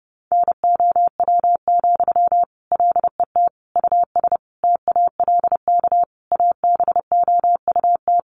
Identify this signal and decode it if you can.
Morse code, heard as a keyed tone, about 30 wpm, 700 Hz. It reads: NOW, LET US TALK ABOUT
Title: Sentences from Top 800 Words 30wpm